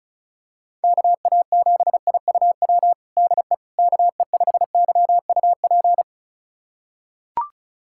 Morse code, heard as a keyed tone, about 34 wpm, 700 Hz. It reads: KA7IUW DE KE5YUP
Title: USA Callsign DE Callsign - 6 Characters - 35wpm